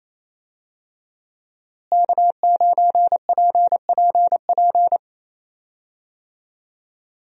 Morse code, heard as a keyed tone, about 28 wpm, 700 Hz. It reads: K9PPP
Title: USA Call Signs - 5 Characters - Repeated 3x - 28wpm